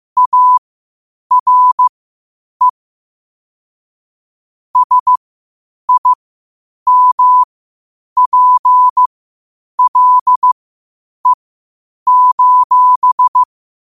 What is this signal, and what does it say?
Morse code, heard as a keyed tone, about 15 wpm, spaced as for 8 wpm, 1000 Hz. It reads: ARE SIMPLE:
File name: qotdfiles / 8wpm